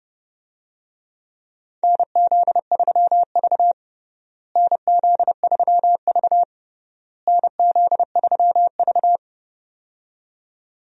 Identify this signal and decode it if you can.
Morse code, heard as a keyed tone, about 30 wpm, 700 Hz. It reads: NZ3V NZ3V NZ3V
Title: USA Call Signs - 4 Characters - Repeated 3x - 30wpm